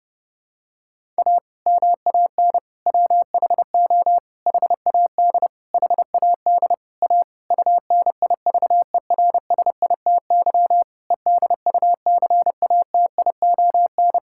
Morse code, heard as a keyed tone, about 30 wpm, 700 Hz. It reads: A MAN WHO HAD HAD A UNIVERSITY EDUCATION
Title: Sentences from Top 1000 Words - Review Part 6 - 30wpm